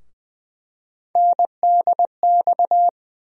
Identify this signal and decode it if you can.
Morse code, heard as a keyed tone, about 21 wpm, 700 Hz. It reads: NDX
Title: USA Call Sign Postfixes - 3 Letters 20wpm